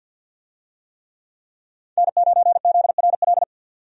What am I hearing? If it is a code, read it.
Morse code, heard as a keyed tone, about 51 wpm, 700 Hz. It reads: N9ZRL
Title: USA Call Signs - 5 Characters - 50wpm